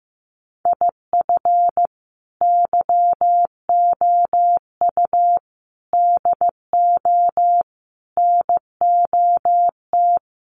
Morse code, heard as a keyed tone, about 15 wpm, 700 Hz. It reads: IF YOU DO NOT
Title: Sets of 5 Words 15wpm